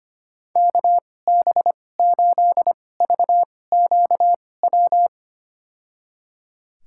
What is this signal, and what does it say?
Morse code, heard as a keyed tone, about 25 wpm, spaced as for 18 wpm, 700 Hz. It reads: KB8VQW
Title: USA Call Signs - 6 Characters - 2x Character Spacing - 25wpm